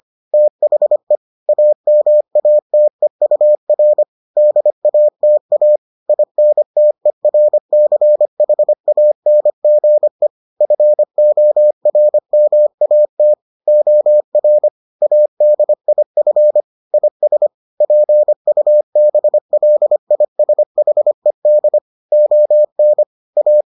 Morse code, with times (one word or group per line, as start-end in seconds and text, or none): 0.3-1.2 THE
1.5-4.0 AMATEUR
4.4-5.8 DATA
6.1-10.3 INTERCHANGE
10.6-13.3 FORMAT
13.7-14.7 OR
15.0-16.6 ADIF
16.9-17.5 IS
17.8-21.8 PUBLISHED
22.1-23.0 ON
23.4-23.6 A